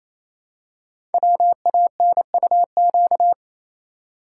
Morse code, heard as a keyed tone, about 28 wpm, 700 Hz. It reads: WANUQ
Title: Code Groups - 5 Letter:Numbers - 28wpm